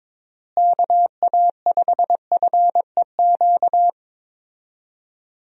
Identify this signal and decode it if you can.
Morse code, heard as a keyed tone, about 22 wpm, 700 Hz. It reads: KA5FEQ
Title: USA Call Signs - 6 Characters - Repeated 3x - 22wpm